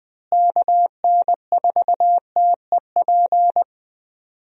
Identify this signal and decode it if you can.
Morse code, heard as a keyed tone, about 20 wpm, 700 Hz. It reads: KN4TEP
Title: USA Call Signs - 6 Characters - Repeated 3x - 20wpm